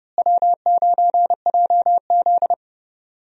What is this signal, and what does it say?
Morse code, heard as a keyed tone, about 30 wpm, 700 Hz. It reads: W9JZ
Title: USA Call Signs - 4 Characters - Repeated 2x - 30wpm